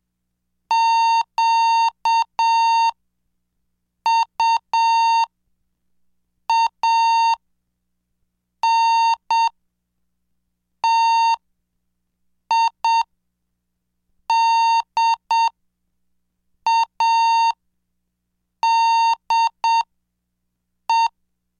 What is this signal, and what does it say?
Morse code, heard as a keyed tone, about 7 wpm, spaced as for 5 wpm, 920 Hz. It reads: QUANTIDADE
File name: cursos / CDestudo / 20 Track 20